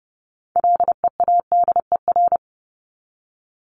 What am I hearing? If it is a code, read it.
Morse code, heard as a keyed tone, about 30 wpm, 700 Hz. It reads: LEADER
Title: Top 1501-1750 Words 30wpm